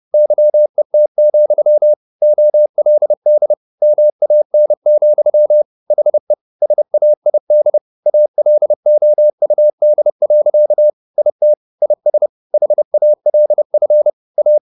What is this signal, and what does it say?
Morse code, heard as a keyed tone, about 30 wpm, 600 Hz. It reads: YET, OLD MAN, HE SAID ALOUD. IT IS HALF A